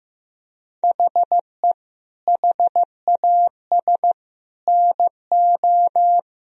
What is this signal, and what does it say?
Morse code, heard as a keyed tone, about 15 wpm, 700 Hz. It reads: HE HAS NO
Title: Sentences from Top 1000 Words - Review Part 4 - 15wpm